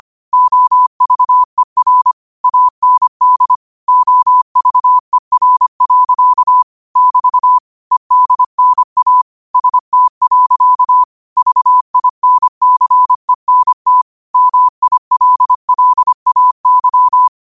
Morse code, heard as a keyed tone, about 25 wpm, 1000 Hz. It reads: OVER AND OVER. <BT> EDNA ST. VINCENT MILLAY
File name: qotdfiles / 25wpm